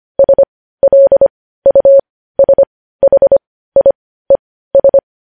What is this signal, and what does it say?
Morse code, heard as a keyed tone, about 25 wpm, spaced as for 15 wpm, 550 Hz. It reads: SLUSHIES
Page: news-headlines-in-morse-code-at-15-wpm